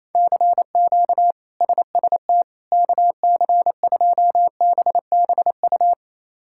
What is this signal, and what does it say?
Morse code, heard as a keyed tone, about 28 wpm, 700 Hz. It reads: CQ SST KC2BBU
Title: K1USN SST QSO Practice - 28wpm